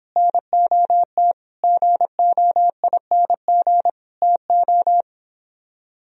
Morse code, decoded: NOT GOING TO

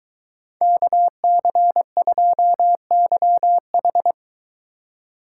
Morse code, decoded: KC2YH

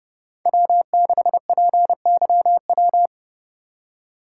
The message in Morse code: W6PYW

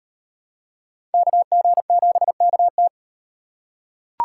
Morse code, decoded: KG7KT

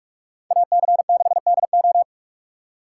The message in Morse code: AC6DO

